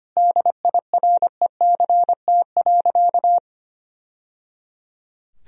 Morse code, heard as a keyed tone, about 25 wpm, 700 Hz. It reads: DIRECT.
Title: Sentences from Top 1000 Words - Review Part 2 - 25wpm